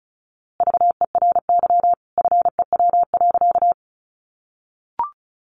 Morse code, heard as a keyed tone, about 34 wpm, 700 Hz. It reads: VERY FEW.
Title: A Sound Of Thunder 35wpm